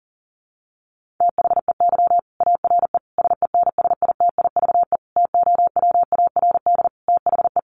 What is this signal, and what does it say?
Morse code, heard as a keyed tone, about 40 wpm, 700 Hz. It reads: THEY ARE SENSITIVE TOWARD THE
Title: Sentences from Top 1400 Words 40wpm